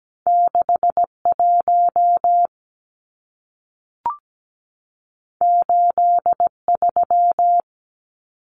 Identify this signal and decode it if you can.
Morse code, heard as a keyed tone, about 17 wpm, 700 Hz. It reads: 61 83
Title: Numbers - 2-Digit 17wpm